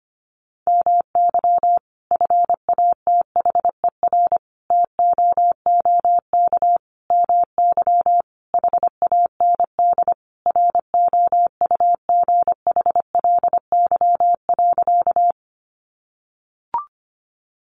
Morse code, heard as a keyed tone, about 25 wpm, 700 Hz. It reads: MY FATHER TOOK MY HAND ROUGHLY.
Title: Sentences from Top 1500 Words 25wpm